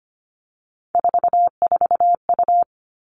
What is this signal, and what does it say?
Morse code, heard as a keyed tone, about 25 wpm, 700 Hz. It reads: 44U
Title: Three Challenging Letter-Number Combinations 25wpm